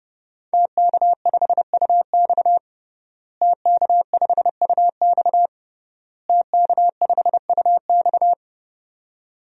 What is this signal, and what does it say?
Morse code, heard as a keyed tone, about 32 wpm, 700 Hz. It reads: TK5UX TK5UX TK5UX
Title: DXCC-100 Call Signs - 5 Characters - Repeated 3x - 30wpm